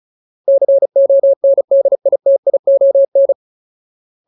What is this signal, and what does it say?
Morse code, heard as a keyed tone, about 34 wpm, 550 Hz. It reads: CONDITION
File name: Ultimate Word List / 35WPM Ultimate